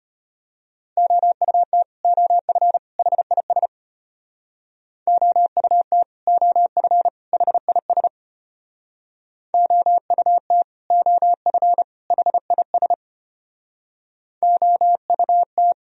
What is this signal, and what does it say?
Morse code, heard as a keyed tone, about 33 wpm, 700 Hz. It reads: OUT OF HIS OUT OF HIS OUT OF HIS OUT